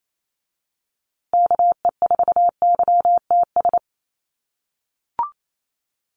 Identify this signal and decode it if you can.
Morse code, heard as a keyed tone, about 28 wpm, 700 Hz. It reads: KE4YTS
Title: USA Call Signs Repeated 3x 28wpm